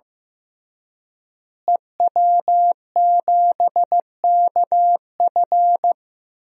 Morse code, heard as a keyed tone, about 15 wpm, 700 Hz. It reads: EW7KF
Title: DXCC-100 Call Signs - 5 Characters - Repeated 2x - 15wpm